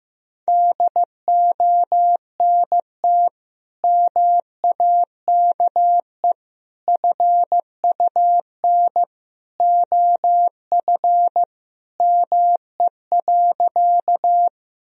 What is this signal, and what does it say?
Morse code, heard as a keyed tone, about 15 wpm, 700 Hz. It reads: DONT MAKE FUN OF ME.
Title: Sentences from Top 1000 Words - Review Part 4 - 15wpm